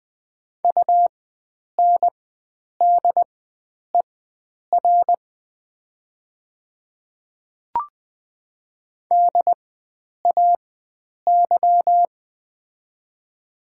Morse code, heard as a keyed tone, about 20 wpm, spaced as for 9 wpm, 700 Hz. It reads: UNDER DAY
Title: 4x Character Spacing - Top 500 Words - 20wpm